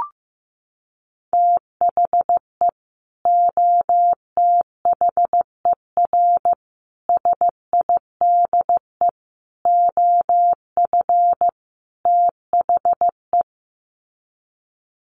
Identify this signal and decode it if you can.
Morse code, heard as a keyed tone, about 15 wpm, 700 Hz. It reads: THE OTHER SIDE OF THE
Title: Sets of 5 Words 15wpm